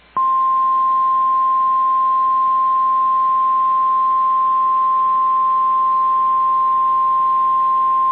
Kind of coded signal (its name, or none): Emergency Alert System (SAME) tones